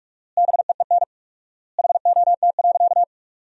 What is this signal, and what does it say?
Morse code, heard as a keyed tone, about 46 wpm, 700 Hz. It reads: BEEN HOT.
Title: 4x Word-Spacing - Sentences from Top 500 Words - 45wpm